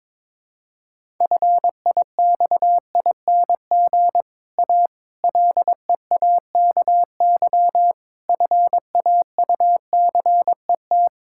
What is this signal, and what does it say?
Morse code, heard as a keyed tone, about 22 wpm, 700 Hz. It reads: FIXING A LEAKY FAUCET
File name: Difficult Sentences with Hints - 22wpm